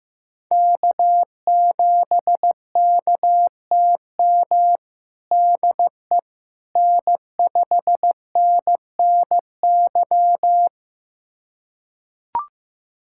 Morse code, heard as a keyed tone, about 16 wpm, 700 Hz. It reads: K7KTM DE N5NNY
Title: USA Callsign DE Callsign 15wpm